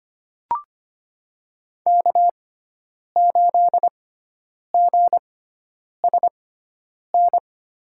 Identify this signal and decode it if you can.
Morse code, heard as a keyed tone, about 25 wpm, spaced as for 9 wpm, 700 Hz. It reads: K8GSN
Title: USA Call Signs - 5 Characters - 6x Character Spacing - 25wpm